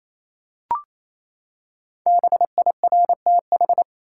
Morse code, heard as a keyed tone, about 28 wpm, 700 Hz. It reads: BIRTH
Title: Five Letter Words - 28wpm